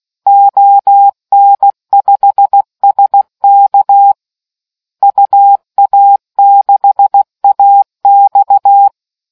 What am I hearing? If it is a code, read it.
Morse code, heard as a keyed tone, about 16 wpm, 800 Hz. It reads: ON5SK UA6AX